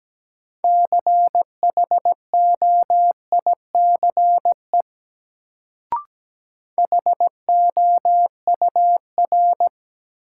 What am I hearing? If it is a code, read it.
Morse code, heard as a keyed tone, about 17 wpm, 700 Hz. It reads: CHOICE HOUR
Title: Sets of 5 Words - New Word Drill - 17wpm